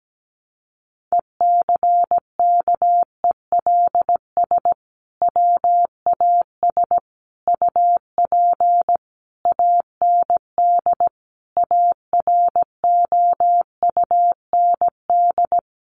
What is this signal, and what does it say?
Morse code, heard as a keyed tone, about 17 wpm, 700 Hz. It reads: ECKELS WAS UP AND AROUND
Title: A Sound Of Thunder 17wpm